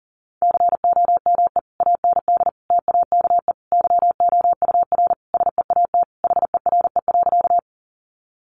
Morse code, heard as a keyed tone, about 40 wpm, 700 Hz. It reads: COME AND TAKE YOUR SEAT HERE.
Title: Sentences from Top 1500 Words 40wpm